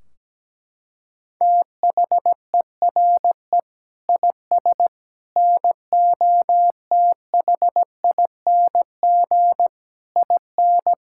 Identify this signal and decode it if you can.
Morse code, heard as a keyed tone, about 17 wpm, 700 Hz. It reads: THERE IS NOTHING IN
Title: Sentences from Top 1600 Words 17wpm